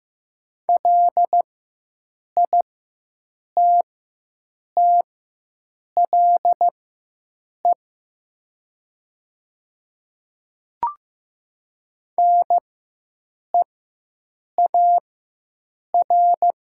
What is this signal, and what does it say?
Morse code, heard as a keyed tone, about 15 wpm, spaced as for 7 wpm, 700 Hz. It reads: LITTLE NEAR
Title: 4x Character Spacing - Top 200 Words - 15wpm